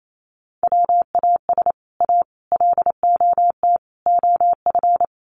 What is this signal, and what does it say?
Morse code, heard as a keyed tone, about 28 wpm, 700 Hz. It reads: WAS A LOT OF
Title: Sets of 4 Words 28wpm